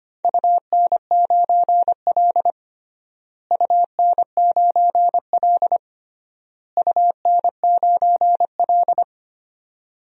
Morse code, heard as a keyed tone, about 25 wpm, 700 Hz. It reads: UN9L UN9L UN9L
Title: DXCC-100 Call Signs - 4 Characters - Repeated 3x - 25wpm